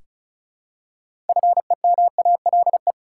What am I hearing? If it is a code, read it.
Morse code, heard as a keyed tone, about 34 wpm, 700 Hz. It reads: FEMALE
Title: Six Letter Words - 35wpm